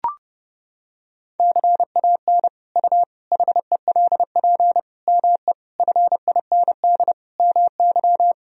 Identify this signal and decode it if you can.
Morse code, heard as a keyed tone, about 30 wpm, 700 Hz. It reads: CAN U HELP ME FIND MY
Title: Sentences from Top 200 Words and Abbreviations 30wpm